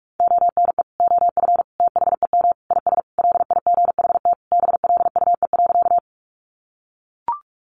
Morse code, heard as a keyed tone, about 44 wpm, 700 Hz. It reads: ONE OF THEM IS LIGHT BLUE.
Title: Additional Easy Phrases 45wpm